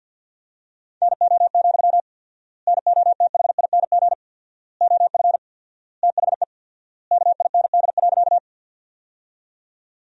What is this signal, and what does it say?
Morse code, heard as a keyed tone, about 50 wpm, 700 Hz. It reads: NO, NOTHING OF THE KIND.
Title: 4x Word-Spacing - Sentences from Top 500 Words - 50wpm